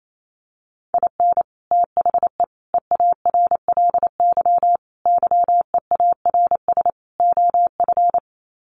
Morse code, heard as a keyed tone, about 28 wpm, 700 Hz. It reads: IN THE EARLY YEARS OF
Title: Sets of 5 Words Encore 28wpm